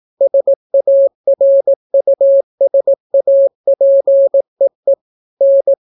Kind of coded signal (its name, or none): Morse code